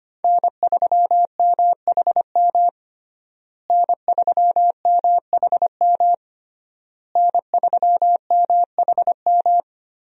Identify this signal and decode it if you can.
Morse code, heard as a keyed tone, about 25 wpm, 700 Hz. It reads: N3MHM N3MHM N3MHM
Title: USA Call Signs - 5 Characters - Repeated 3x - 25wpm